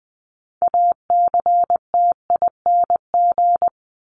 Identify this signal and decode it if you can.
Morse code, heard as a keyed tone, about 20 wpm, 700 Hz. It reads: ACTING
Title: ING Suffix - Top 100 Words with Base Words 20wpm